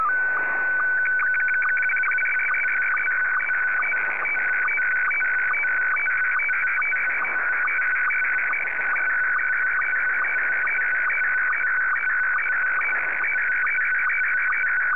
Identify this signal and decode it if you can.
SSTV (slow-scan television), Scottie 1.